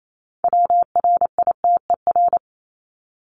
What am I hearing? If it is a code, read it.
Morse code, heard as a keyed tone, about 28 wpm, 700 Hz. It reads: WRITER
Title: Top 1751-2000 Words 28wpm